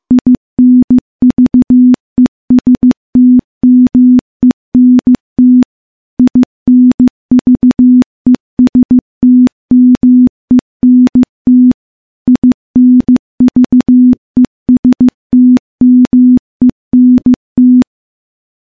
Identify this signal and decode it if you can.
Morse code, heard as a keyed tone, about 15 wpm, 250 Hz. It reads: INVESTMENT INVESTMENT INVESTMENT
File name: morse / morse-training-15WPM